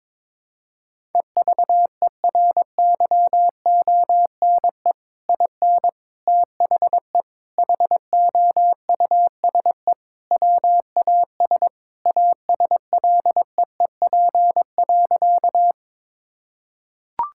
Morse code, heard as a keyed tone, about 21 wpm, 700 Hz. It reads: EVERYONE IN THE HOUSE WAS ASLEEP.
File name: practice / 22wpm / Sentences with 3-Word Phrases - 22wpm